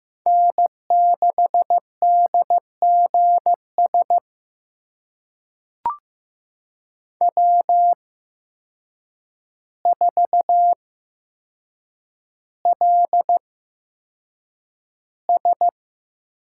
Morse code, heard as a keyed tone, about 15 wpm, spaced as for 7 wpm, 700 Hz. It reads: N6DGS W4LS
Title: USA Call Signs - 5 Characters - 8x Character Spacing - 15wpm